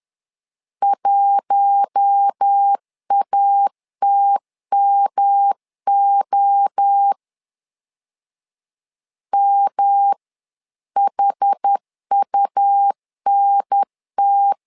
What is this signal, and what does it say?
Morse code, heard as a keyed tone, about 10 wpm, 800 Hz. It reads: 1ATMO MHUNT
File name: LEZIONI / 4 Q1 10wpm 2448